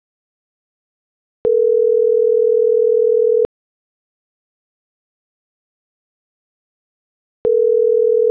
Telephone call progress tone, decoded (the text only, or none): ringback tone